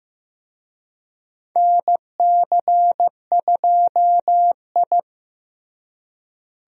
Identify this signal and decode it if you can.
Morse code, heard as a keyed tone, about 15 wpm, 700 Hz. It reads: NC2I
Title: USA Call Signs - 4 Characters - Repeated 2x - 15wpm